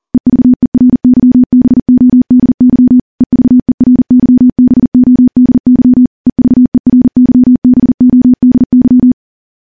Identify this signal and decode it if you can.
Morse code, heard as a keyed tone, about 39 wpm, 250 Hz. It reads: EVERYBODY EVERYBODY EVERYBODY
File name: morse / morse-training-40WPM